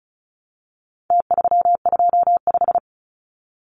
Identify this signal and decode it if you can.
Morse code, heard as a keyed tone, about 34 wpm, 700 Hz. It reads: T325
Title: Numbers - Contest Style - 4-Digit 35wpm